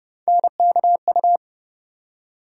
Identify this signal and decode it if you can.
Morse code, heard as a keyed tone, about 30 wpm, 700 Hz. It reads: NKU
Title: USA Call Sign Postfixes - 3 Letters 30wpm